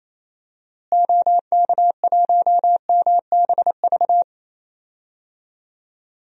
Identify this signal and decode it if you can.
Morse code, heard as a keyed tone, about 28 wpm, 700 Hz. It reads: OK1MBV
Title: DXCC-100 Call Signs - 6 Characters - Repeated 3x - 28wpm